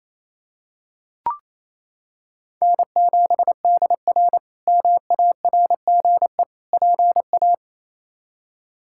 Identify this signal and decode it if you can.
Morse code, heard as a keyed tone, about 28 wpm, 700 Hz. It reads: N7DR MARGE PA
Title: K1USN SST QSO Practice - 28wpm